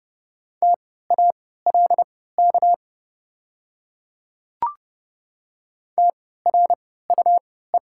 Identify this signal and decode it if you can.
Morse code, heard as a keyed tone, about 30 wpm, spaced as for 17 wpm, 700 Hz. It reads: TALK TRUE
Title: 3x Character Spacing - Top 500 Words - 30wpm